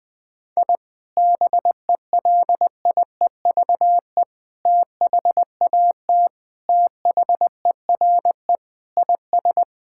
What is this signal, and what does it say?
Morse code, heard as a keyed tone, about 19 wpm, 700 Hz. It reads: I BELIEVE THAT THERE IS
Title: Sentences with 3-Word Phrases - Spoken Beginning - 20wpm